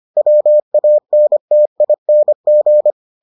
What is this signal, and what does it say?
Morse code, heard as a keyed tone, about 25 wpm, 600 Hz. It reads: WANTING